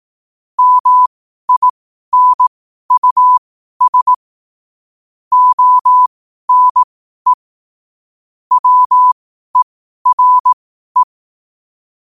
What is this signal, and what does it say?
Morse code, heard as a keyed tone, about 18 wpm, spaced as for 12 wpm, 1000 Hz. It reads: MINUS ONE WERE
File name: qotdfiles / 12wpm